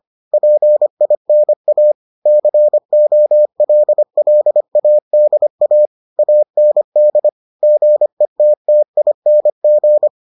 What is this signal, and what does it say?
Morse code, heard as a keyed tone, about 25 wpm, 600 Hz. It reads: PINA COLLADA AND GETTING